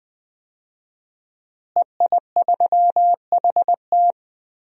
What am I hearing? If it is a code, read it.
Morse code, heard as a keyed tone, about 20 wpm, 700 Hz. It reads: EI3HT
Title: DXCC-100 Call Signs - 5 Characters - Repeated 3x - 20wpm